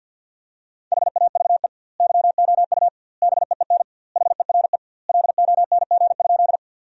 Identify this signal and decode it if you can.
Morse code, heard as a keyed tone, about 52 wpm, 700 Hz. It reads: HAVE YOU BEEN HERE LONG?
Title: 2x Word-Spacing - Sentences from Top 200 Words - 50wpm